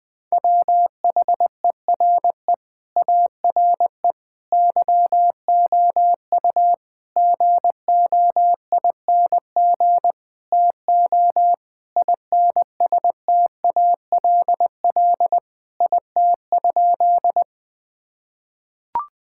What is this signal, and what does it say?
Morse code, heard as a keyed tone, about 21 wpm, 700 Hz. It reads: WHERE ARE YOU GOING TO INSTALL IT?
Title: Sentences from Top 1600 Words 20wpm